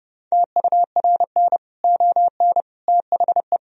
Morse code, heard as a keyed tone, about 32 wpm, 700 Hz. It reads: TURN ON THE
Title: Sentences from Top 1000 Words - Review Part 4 - 30wpm